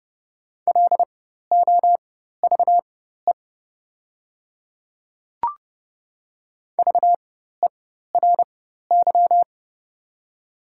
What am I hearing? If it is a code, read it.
Morse code, heard as a keyed tone, about 30 wpm, spaced as for 14 wpm, 700 Hz. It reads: LOVE VERY
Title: 4x Character Spacing - Top 500 Words - 30wpm